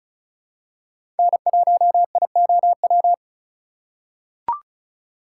Morse code, heard as a keyed tone, about 36 wpm, 700 Hz. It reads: N1IOW
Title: USA Call Signs - 5 Characters - 2x Character Spacing - 35wpm